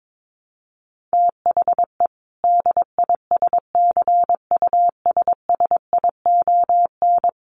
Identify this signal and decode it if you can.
Morse code, heard as a keyed tone, about 22 wpm, 700 Hz. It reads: THE DISCUSSION
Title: Sentences from Top 1100 Words 22wpm